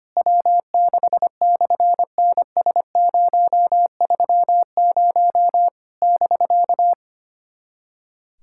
Morse code, heard as a keyed tone, about 25 wpm, 700 Hz. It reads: W6/NS030 <BK>
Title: SOTA QSO Practice 25wpm